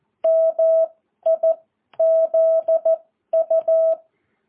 Morse code, heard as a keyed tone, about 14 wpm, spaced as for 11 wpm, 650 Hz. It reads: MIZU
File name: jh8cbh